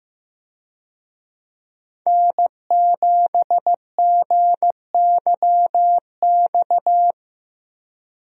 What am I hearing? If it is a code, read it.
Morse code, heard as a keyed tone, about 15 wpm, 700 Hz. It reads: N7GYX
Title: USA Call Signs - 5 Characters - Repeated 3x - 15wpm